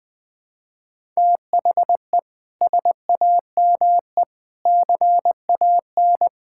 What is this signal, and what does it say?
Morse code, heard as a keyed tone, about 20 wpm, 700 Hz. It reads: THE SAME CAN
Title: Sentences with 6-Word Phrases - Spoken Beginning - 20wpm